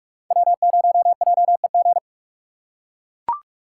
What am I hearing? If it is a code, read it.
Morse code, heard as a keyed tone, about 41 wpm, 700 Hz. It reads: W0JEG